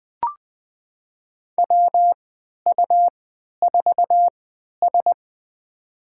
Morse code, heard as a keyed tone, about 20 wpm, spaced as for 11 wpm, 700 Hz. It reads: WU4S
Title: USA Call Signs - 4 Characters - 3x Character Spacing - 20wpm